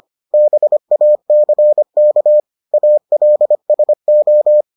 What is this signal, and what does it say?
Morse code, heard as a keyed tone, about 25 wpm, 600 Hz. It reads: BACK ALSO